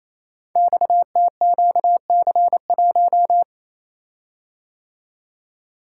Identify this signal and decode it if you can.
Morse code, heard as a keyed tone, about 27 wpm, 700 Hz. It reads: XTQC1
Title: Code Groups - 5 Letter:Numbers - 28wpm